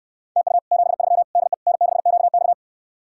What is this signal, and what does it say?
Morse code, heard as a keyed tone, about 69 wpm, 700 Hz. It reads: TU 73 DE N6QX